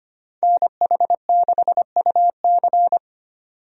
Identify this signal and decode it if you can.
Morse code, heard as a keyed tone, about 25 wpm, 690 Hz. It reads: NH6UC
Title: USA Call Signs - 5 Characters - 25wpm